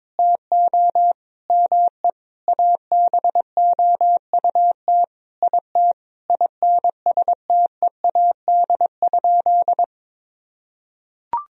Morse code, heard as a keyed tone, about 22 wpm, 700 Hz. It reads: TO ME ABOUT IT INSTEAD?
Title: Sentences from Top 1000 Words - Review Part 4 - 22wpm